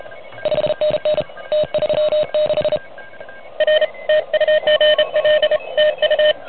Morse code, heard as a keyed tone, about 34 wpm, 590 Hz. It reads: 5NN T36 R TU GL TU
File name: truSDX / MWC29.8